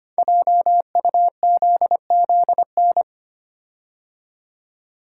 Morse code, heard as a keyed tone, about 25 wpm, 700 Hz. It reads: JUZZN
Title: Code Groups - 5 Letters - 25wpm